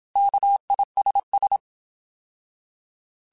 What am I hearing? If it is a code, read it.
Morse code, heard as a keyed tone, about 27 wpm, 800 Hz. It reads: KISS
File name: Alphabetical / Alphabetical mp3s / Ks @25